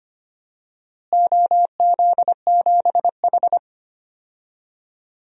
Morse code, heard as a keyed tone, about 25 wpm, 700 Hz. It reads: OZ7H
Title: DXCC-100 Call Signs - 4 Characters - Repeated 2x - 25wpm